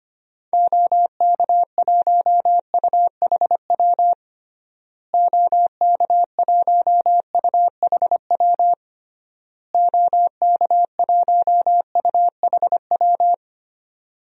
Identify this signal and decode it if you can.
Morse code, heard as a keyed tone, about 25 wpm, 700 Hz. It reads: OK1UHW OK1UHW OK1UHW
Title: DXCC-100 Call Signs - 6 Characters - Repeated 3x - 25wpm